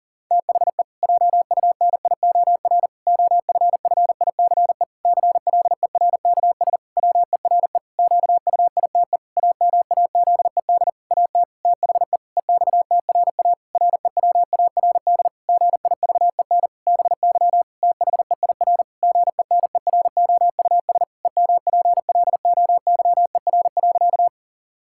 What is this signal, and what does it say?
Morse code, heard as a keyed tone, about 40 wpm, 700 Hz. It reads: THE JUNIOR OFFICE CLERKS WERE QUITE AMAZED AT THE EXTRA REWARD GIVEN BY THEIR GENEROUS EMPLOYER.
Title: Pangram Challenge v2 - 40wpm